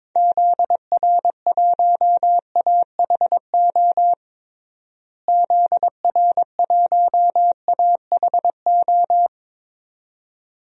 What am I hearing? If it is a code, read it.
Morse code, heard as a keyed tone, about 22 wpm, 700 Hz. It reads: ZR1AHO ZR1AHO
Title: DXCC-100 Call Signs - 6 Characters - Repeated 2x - 22wpm